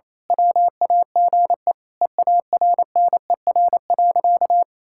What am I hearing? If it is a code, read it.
Morse code, heard as a keyed tone, about 28 wpm, 700 Hz. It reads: WAGE EARNER.
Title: Pangram Challenge v2 - 28wpm